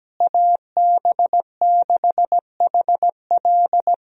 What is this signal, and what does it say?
Morse code, heard as a keyed tone, about 17 wpm, 700 Hz. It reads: AB6HL
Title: USA Call Signs - 5 Characters - Repeated 3x - 17wpm